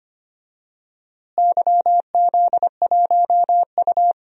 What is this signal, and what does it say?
Morse code, heard as a keyed tone, about 25 wpm, 700 Hz. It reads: YZ1U
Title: DXCC-100 Call Signs - 4 Characters - Repeated 2x - 25wpm